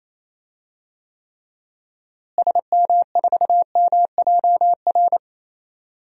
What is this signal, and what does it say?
Morse code, heard as a keyed tone, about 28 wpm, 700 Hz. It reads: SM4MJR
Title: DXCC-100 Call Signs - 6 Characters - Repeated 2x - 28wpm